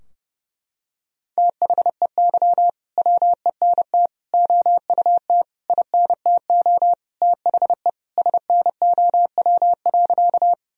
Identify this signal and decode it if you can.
Morse code, heard as a keyed tone, about 32 wpm, 700 Hz. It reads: THEY WENT OUT INTO THE SNOW.
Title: Sentences from Top 1000 Words - Review Part 3 - 30wpm